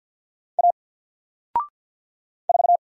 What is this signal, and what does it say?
Morse code, heard as a keyed tone, about 50 wpm, 700 Hz. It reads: A 4
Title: Single Letter-Number 50wpm